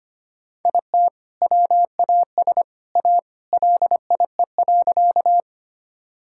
Morse code, heard as a keyed tone, about 25 wpm, 700 Hz. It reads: IT WAS A LIE.
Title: Sentences from Top 1600 Words 25wpm